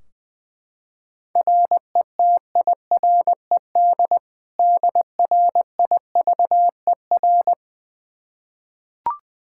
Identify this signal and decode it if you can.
Morse code, heard as a keyed tone, about 20 wpm, 700 Hz. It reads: RETIRED DRIVER
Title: K1USN SST QSO Practice 20wpm